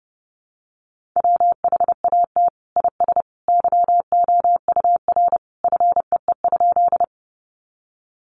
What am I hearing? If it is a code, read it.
Morse code, heard as a keyed tone, about 30 wpm, 700 Hz. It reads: WHAT IS YOUR FEE?
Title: Sentences from Top 1500 Words 30wpm